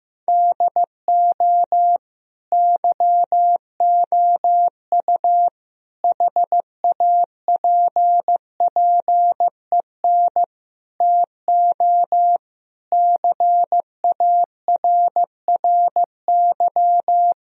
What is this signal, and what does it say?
Morse code, heard as a keyed tone, about 16 wpm, 700 Hz. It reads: DO YOU HAPPEN TO CARRY
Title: Sentences from Top 1000 Words - Review Part 2 - 15wpm